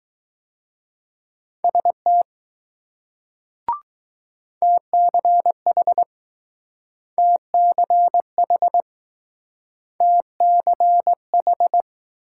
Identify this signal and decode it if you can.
Morse code, heard as a keyed tone, about 20 wpm, 700 Hz. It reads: ST TCH TCH TCH